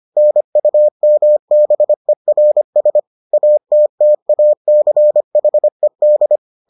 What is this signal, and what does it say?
Morse code, heard as a keyed tone, about 25 wpm, 600 Hz. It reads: NUMBERS ATTACHED